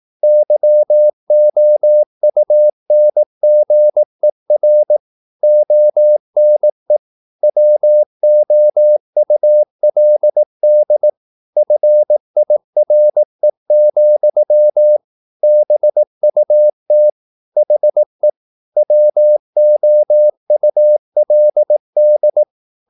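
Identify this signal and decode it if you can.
Morse code, heard as a keyed tone, about 18 wpm, 600 Hz. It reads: YOUNGER ONE WOULD FIRE, BUT HE WOULD
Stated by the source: Morse practice at 18 wpm, English text, from CW med Gnister.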